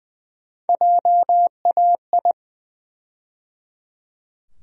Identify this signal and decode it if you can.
Morse code, heard as a keyed tone, about 20 wpm, 700 Hz. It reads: JAI
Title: USA Call Sign Postfixes - 3 Letters 20wpm